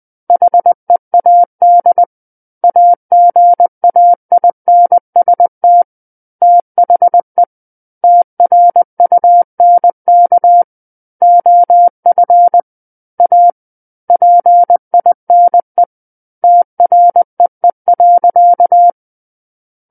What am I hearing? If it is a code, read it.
Morse code, heard as a keyed tone, about 20 wpm, 700 Hz. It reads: HEAD AGAINST THE TRUNK OF A PINE TREE.
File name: 20 WPM